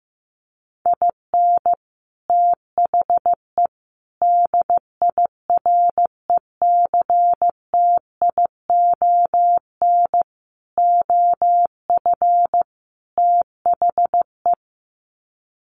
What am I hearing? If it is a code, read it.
Morse code, heard as a keyed tone, about 15 wpm, 700 Hz. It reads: IN THE DIRECTION OF THE
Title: Sets of 5 Words 15wpm